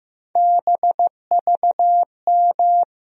Morse code, heard as a keyed tone, about 15 wpm, 700 Hz. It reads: BVM
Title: USA Call Sign Postfixes - 3 Letters 15wpm